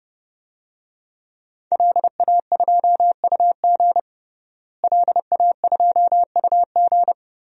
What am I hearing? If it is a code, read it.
Morse code, heard as a keyed tone, about 32 wpm, 700 Hz. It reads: LA2UG LA2UG
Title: DXCC-100 Call Signs - 5 Characters - Repeated 2x - 30wpm